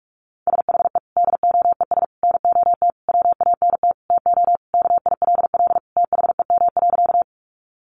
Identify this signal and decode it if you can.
Morse code, heard as a keyed tone, about 44 wpm, 700 Hz. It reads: SHE DOES NOT WANT TO KILL THEM.
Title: Additional Easy Phrases 45wpm